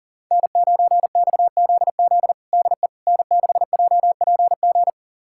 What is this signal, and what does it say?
Morse code, heard as a keyed tone, about 40 wpm, 700 Hz. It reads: N9XZZ DE N6JPG
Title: USA Callsign DE Callsign - 5 Characters - 40wpm